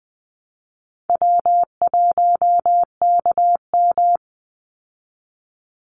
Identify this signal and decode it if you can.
Morse code, heard as a keyed tone, about 20 wpm, 700 Hz. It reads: W1KM
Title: Call Signs Repeated 3x - 20wpm